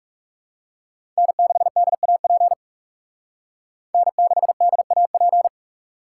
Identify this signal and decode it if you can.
Morse code, heard as a keyed tone, about 38 wpm, 700 Hz. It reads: N6DAP N6DAP